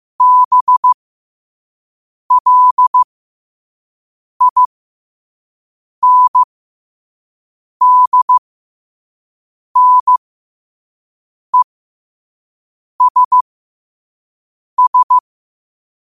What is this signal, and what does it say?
Morse code, heard as a keyed tone, about 15 wpm, spaced as for 5 wpm, 1000 Hz. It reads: BLINDNESS
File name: qotdfiles / 5wpm